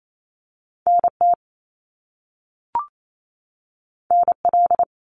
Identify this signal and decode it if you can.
Morse code, heard as a keyed tone, about 28 wpm, 700 Hz. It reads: NT NL
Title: Canadian Provinces and Territories 28wpm